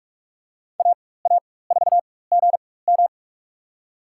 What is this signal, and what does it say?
Morse code, heard as a keyed tone, about 46 wpm, spaced as for 21 wpm, 700 Hz. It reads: AA4GM
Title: USA Call Signs - 5 Characters - 4x Character Spacing - 45wpm